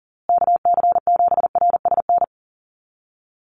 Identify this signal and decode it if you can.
Morse code, heard as a keyed tone, about 40 wpm, 700 Hz. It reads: KC7RSN
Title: USA Call Signs Repeated 2x 40wpm